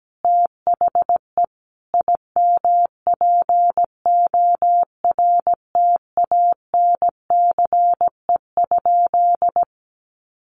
Morse code, heard as a keyed tone, about 17 wpm, 700 Hz. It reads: THE IMPORTANCE?
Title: Sentences from Top 1200 Words 17wpm